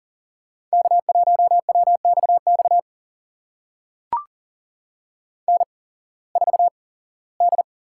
Morse code, heard as a keyed tone, about 40 wpm, spaced as for 21 wpm, 700 Hz. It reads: K1WXX N4D
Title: USA Call Signs - 5 Characters - 8x Character Spacing - 40wpm